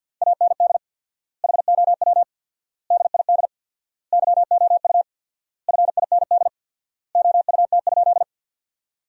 Morse code, heard as a keyed tone, about 48 wpm, 700 Hz. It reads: AND HOW DID YOU FIND OUT?
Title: 4x Word-Spacing - Sentences from Top 200 Words - 50wpm